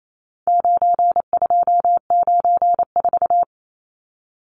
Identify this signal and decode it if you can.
Morse code, heard as a keyed tone, about 28 wpm, 700 Hz. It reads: 9294
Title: Numbers - 4-Digit 28wpm